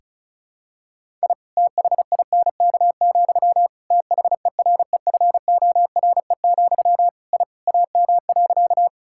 Morse code, heard as a keyed tone, about 34 wpm, 700 Hz. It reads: I THINK, THEREFORE, I AM.
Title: Sentences from Top 1000 Words - Review Part 5 - 35wpm